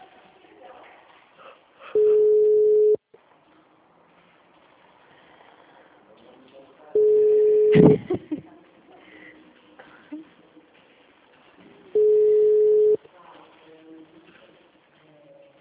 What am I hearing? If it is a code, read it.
Ringback tone (Europe, 425 Hz), with speech.